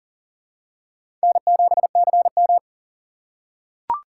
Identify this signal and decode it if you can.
Morse code, heard as a keyed tone, about 40 wpm, 700 Hz. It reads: N7CM
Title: USA Call Signs - 4 Characters - 3x Character Spacing - 40wpm